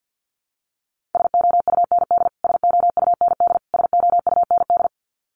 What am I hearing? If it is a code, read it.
Morse code, heard as a keyed tone, about 50 wpm, 700 Hz. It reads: SOUND SOUND SOUND
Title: Top 500 Words Repeated 50wpm